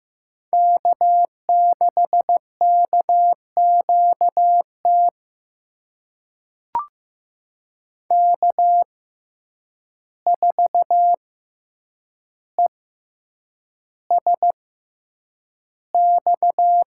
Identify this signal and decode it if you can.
Morse code, heard as a keyed tone, about 15 wpm, spaced as for 8 wpm, 700 Hz. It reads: K6KQT K4ESX
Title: USA Call Signs - 5 Characters - 6x Character Spacing - 15wpm